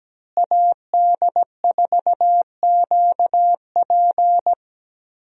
Morse code, heard as a keyed tone, about 17 wpm, 700 Hz. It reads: AD4QP
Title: USA Call Signs - 5 Characters - Repeated 3x - 17wpm